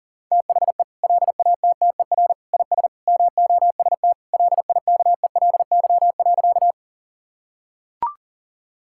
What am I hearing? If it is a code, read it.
Morse code, heard as a keyed tone, about 40 wpm, 700 Hz. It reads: THE LATTER IS MOST LIKELY.
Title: Sentences from Top 1600 Words 40wpm